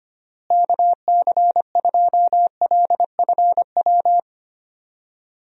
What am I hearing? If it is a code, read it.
Morse code, heard as a keyed tone, about 25 wpm, 700 Hz. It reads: KC2LFW
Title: USA Call Signs - 6 Characters - Repeated 2x - 25wpm